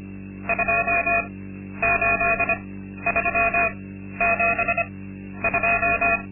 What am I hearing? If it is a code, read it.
Morse code, heard as a keyed tone, about 25 wpm, spaced as for 12 wpm, 680 Hz. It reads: 28372
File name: Linksmakalnis-2